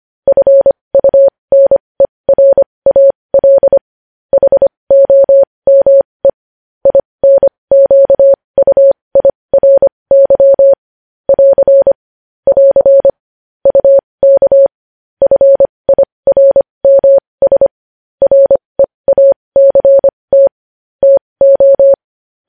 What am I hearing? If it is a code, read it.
Morse code, heard as a keyed tone, about 25 wpm, spaced as for 20 wpm, 550 Hz. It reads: FUNERAL HOME INQUIRY <AR> <AR> UK FIRMS REACT TO
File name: podcast / News headlines-1x-25x20